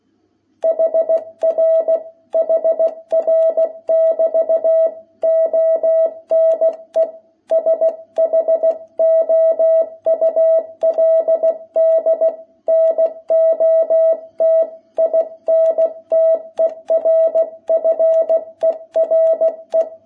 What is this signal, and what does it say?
Morse code, heard as a keyed tone, about 16 wpm, 660 Hz. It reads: HR HR<BT>ONE SHOULD NOT INTERFERE
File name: jh8cbh / 2021cw